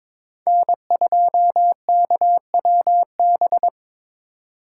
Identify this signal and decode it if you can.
Morse code, heard as a keyed tone, about 22 wpm, 700 Hz. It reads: N2KWB
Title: USA Call Signs - 4x Character Spacing - 22wpm